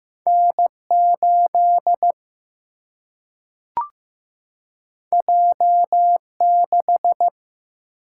Morse code, heard as a keyed tone, about 15 wpm, 700 Hz. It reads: N 8 J 6
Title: Single-Letter Single-Number 15wpm